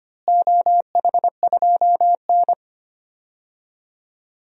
Morse code, heard as a keyed tone, about 25 wpm, 700 Hz. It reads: OH2N
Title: DXCC-100 Call Signs - 4 Characters - Repeated 2x - 25wpm